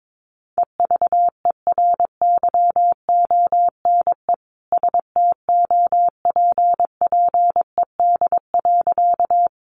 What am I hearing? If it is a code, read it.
Morse code, heard as a keyed tone, about 22 wpm, 700 Hz. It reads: EVERYONE STOPPED.
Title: A Sound Of Thunder 22wpm